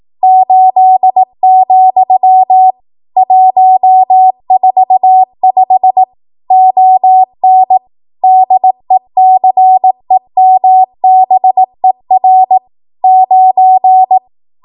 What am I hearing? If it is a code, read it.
Morse code, heard as a keyed tone, about 18 wpm, 750 Hz. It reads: 8, 145 ON DECEMBER 9
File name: Archive / 18 WPM / 18WPM